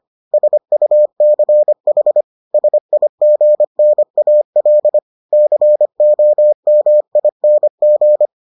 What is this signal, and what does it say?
Morse code, heard as a keyed tone, about 25 wpm, 600 Hz. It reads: SUCH SIGNAL COMING